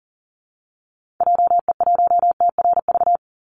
Morse code, heard as a keyed tone, about 40 wpm, 700 Hz. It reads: JE1TRV
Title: Call Signs Repeated 3x - 40wpm